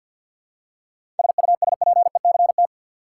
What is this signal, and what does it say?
Morse code, heard as a keyed tone, about 51 wpm, 700 Hz. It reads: SUSPECT
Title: Seven Letter Words - 50wpm